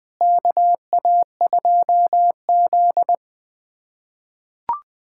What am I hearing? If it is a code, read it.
Morse code, heard as a keyed tone, about 20 wpm, 700 Hz. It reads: KA2Z
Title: USA Callsign DE Callsign - 4 Characters - 20wpm